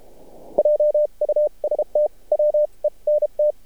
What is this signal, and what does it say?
Morse code, heard as a keyed tone, about 32 wpm, spaced as for 27 wpm, 600 Hz. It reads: JUST WENT
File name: archived / CRSnet